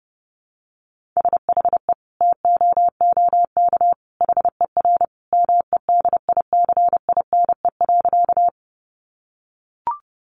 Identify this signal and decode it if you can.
Morse code, heard as a keyed tone, about 28 wpm, 700 Hz. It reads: SHE TOOK HER MEDICINE.
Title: Sentences from Top 900 Words 30wpm